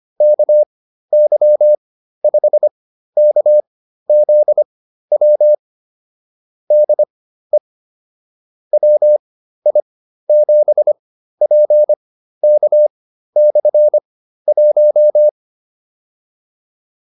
Morse code, heard as a keyed tone, about 25 wpm, spaced as for 13 wpm, 600 Hz. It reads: KY5KZW DE WI7PK/1
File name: QSO304 13